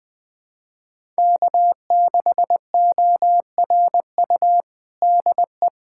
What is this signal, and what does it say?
Morse code, heard as a keyed tone, about 20 wpm, 700 Hz. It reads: K6ORU DE
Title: USA Callsign DE Callsign - 5 Characters - 20wpm